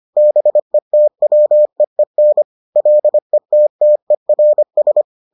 Morse code, heard as a keyed tone, about 25 wpm, 600 Hz. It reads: BETWEEN LETTERS